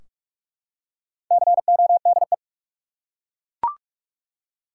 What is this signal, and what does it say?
Morse code, heard as a keyed tone, about 42 wpm, 700 Hz. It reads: CODE
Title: Four Letter Words - 45wpm